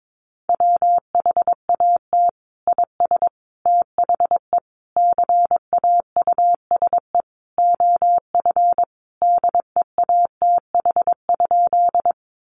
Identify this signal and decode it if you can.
Morse code, heard as a keyed tone, about 22 wpm, 700 Hz. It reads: WHAT IS THE CAUSE OF DEATH?
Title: Sentences from Top 1100 Words 22wpm